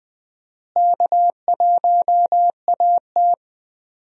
Morse code, heard as a keyed tone, about 20 wpm, 700 Hz. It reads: K1AT
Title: USA Call Signs - 4 Characters - Repeated 3x - 20wpm